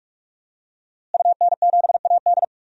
Morse code, heard as a keyed tone, about 46 wpm, 700 Hz. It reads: UN7AD
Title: DXCC-100 Call Signs - 5 Characters - Repeated 3x - 45wpm